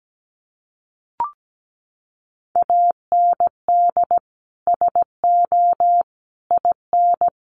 Morse code, heard as a keyed tone, about 17 wpm, 700 Hz. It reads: AND SO IN
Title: Sentences from Top 800 Words 17wpm